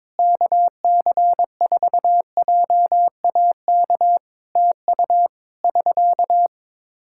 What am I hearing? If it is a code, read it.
Morse code, heard as a keyed tone, about 22 wpm, 700 Hz. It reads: KC4JAK TU <SK>
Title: K1USN SST QSO Practice 22wpm